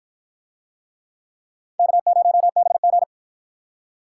Morse code, heard as a keyed tone, about 53 wpm, 700 Hz. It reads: K0BG